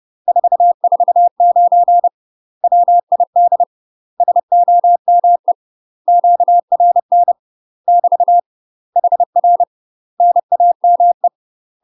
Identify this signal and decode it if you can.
Morse code, heard as a keyed tone, about 30 wpm, 700 Hz. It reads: 449 WID SOME QRN <BT> HR NAME
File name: UK MASTER QSOs TRAINING / 30WPM / UK MASTER QSOs 0007